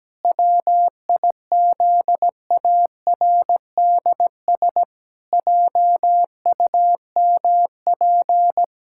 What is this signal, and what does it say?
Morse code, heard as a keyed tone, about 17 wpm, 700 Hz. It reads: WIZARDS JUMP
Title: Pangram Challenge v2 - 17wpm